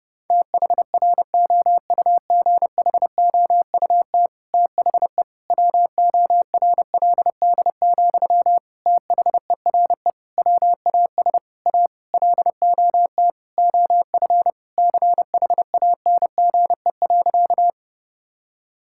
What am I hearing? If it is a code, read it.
Morse code, heard as a keyed tone, about 30 wpm, 700 Hz. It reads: THROUGHOUT THE WORLD, THERE WAS A LOT OF CHANGE.
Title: Sentences from Top 1000 Words - Review Part 5 - 30wpm